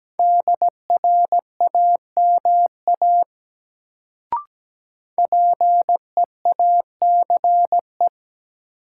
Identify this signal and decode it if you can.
Morse code, heard as a keyed tone, about 17 wpm, 700 Hz. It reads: DRAMA PEACE
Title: Sentences from Top 1500 Words - New Word Drill - 17wpm